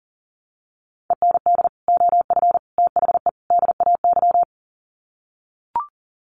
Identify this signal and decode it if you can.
Morse code, heard as a keyed tone, about 37 wpm, 700 Hz. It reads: END OF THE DAY
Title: Sets of 4 Words 40wpm